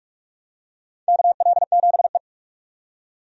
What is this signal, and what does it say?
Morse code, heard as a keyed tone, about 47 wpm, 700 Hz. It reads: KL7E
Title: DXCC-100 Call Signs - 4 Characters - 45wpm